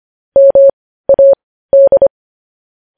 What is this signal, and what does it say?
Morse code, heard as a keyed tone, about 25 wpm, spaced as for 15 wpm, 550 Hz. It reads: MAD